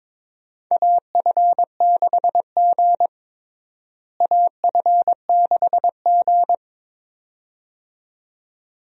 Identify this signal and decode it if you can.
Morse code, heard as a keyed tone, about 22 wpm, 700 Hz. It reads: AF6G AF6G
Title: USA Call Signs - 4 Characters - Repeated 2x - 22wpm